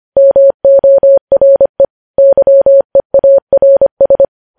Morse code, heard as a keyed tone, about 25 wpm, 550 Hz. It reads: MORE YEARS
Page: news headlines in morse code at 25 wpm